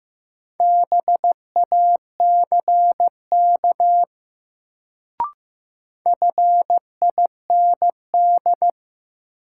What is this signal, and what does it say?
Morse code, heard as a keyed tone, about 15 wpm, 700 Hz. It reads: BACK FIND
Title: Sets of 2 Words - New Word Drill - 15wpm